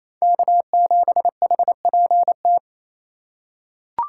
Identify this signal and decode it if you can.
Morse code, heard as a keyed tone, about 28 wpm, 700 Hz. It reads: K7HPT
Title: K1USN SST QSO Practice - 28wpm